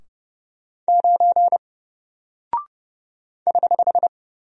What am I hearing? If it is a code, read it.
Morse code, heard as a keyed tone, about 30 wpm, 700 Hz. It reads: 9 <HH>
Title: Single Character (Letter:Number:Punctuation:Prosign) Alt 30wpm